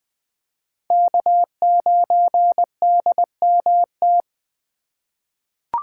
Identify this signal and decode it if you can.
Morse code, heard as a keyed tone, about 20 wpm, 700 Hz. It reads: K9DMT
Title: USA Call Signs - 5 Characters - 2x Character Spacing - 20wpm